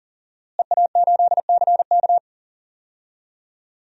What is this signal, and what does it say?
Morse code, heard as a keyed tone, about 40 wpm, 700 Hz. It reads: EA8CK
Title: DXCC-100 Call Signs - 5 Characters - Repeated 2x - 40wpm